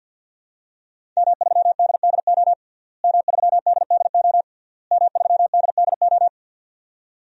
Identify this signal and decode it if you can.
Morse code, heard as a keyed tone, about 46 wpm, 700 Hz. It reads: M3DDO M3DDO M3DDO
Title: DXCC-100 Call Signs - 5 Characters - Repeated 3x - 50wpm